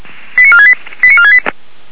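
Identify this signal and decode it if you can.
Five-tone selective call (ZVEI), heard as quick runs of five tones, tones of about 70 ms. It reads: 89468 89468
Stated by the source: ZVEI five-tone call received on SDR.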